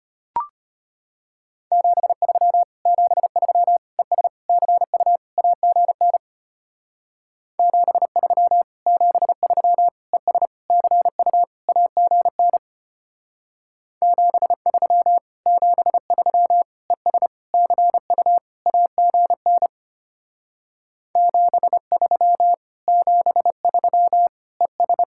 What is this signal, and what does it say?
Morse code, heard as a keyed tone, about 33 wpm, 700 Hz. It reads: 73 73 ES CU AGN 73 73 ES CU AGN 73 73 ES CU AGN 73 73 ES